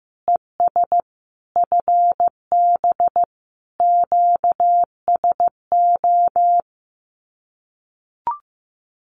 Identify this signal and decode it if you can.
Morse code, heard as a keyed tone, about 15 wpm, 700 Hz. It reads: ES FB QSO
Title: Common QSO Phrases Encore 15wpm